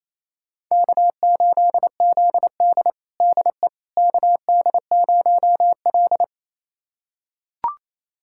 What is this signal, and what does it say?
Morse code, heard as a keyed tone, about 28 wpm, 700 Hz. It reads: K8ZD DE KD0L
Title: USA Callsign DE Callsign - 4 Characters - 28wpm